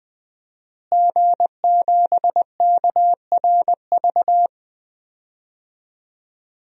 Morse code, heard as a keyed tone, about 20 wpm, 700 Hz. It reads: G7KRV
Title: DXCC-100 Call Signs - 5 Characters - Repeated 2x - 20wpm